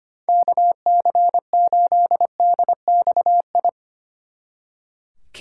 Morse code, heard as a keyed tone, about 33 wpm, 700 Hz. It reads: KC8DXI